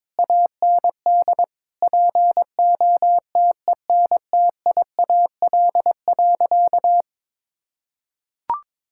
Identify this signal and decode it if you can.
Morse code, heard as a keyed tone, about 22 wpm, 700 Hz. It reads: AND POTENTIAL.
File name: Sentences with 9-Word Phrases - 22wpm